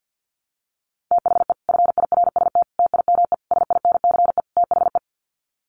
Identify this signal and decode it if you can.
Morse code, heard as a keyed tone, about 50 wpm, 700 Hz. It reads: THE FIRST TIME SINCE THE
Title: Sets of 5 Words Encore 50wpm